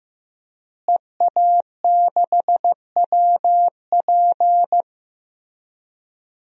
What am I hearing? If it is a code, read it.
Morse code, heard as a keyed tone, about 15 wpm, 700 Hz. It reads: EA6WP
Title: DXCC-100 Call Signs - 5 Characters - Repeated 3x - 15wpm